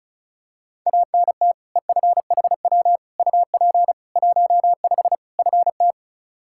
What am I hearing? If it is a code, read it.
Morse code, heard as a keyed tone, about 34 wpm, 700 Hz. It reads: ANT EFHW UP 15 FT
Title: Practice QSO - Exchange 2 - Info - 35wpm